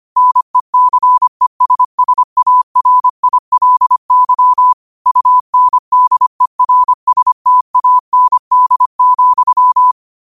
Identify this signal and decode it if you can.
Morse code, heard as a keyed tone, about 25 wpm, 1000 Hz. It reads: NECESSARILY UNDERSTAND,